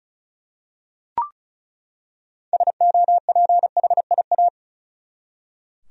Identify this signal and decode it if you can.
Morse code, heard as a keyed tone, about 36 wpm, 700 Hz. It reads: SOPHIA
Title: USA Common Names - 35wpm